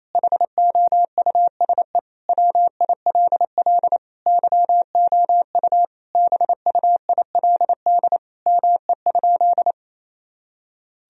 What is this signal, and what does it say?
Morse code, heard as a keyed tone, about 28 wpm, 700 Hz. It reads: HOUSE WILL YOU BUILD ME?
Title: Sentences from Top 1000 Words - Review Part 1 - 28wpm